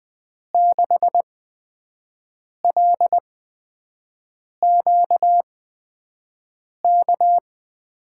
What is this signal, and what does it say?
Morse code, heard as a keyed tone, about 20 wpm, spaced as for 5 wpm, 700 Hz. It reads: 6LQK
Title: USA Call Signs - 8x Character Spacing - 20wpm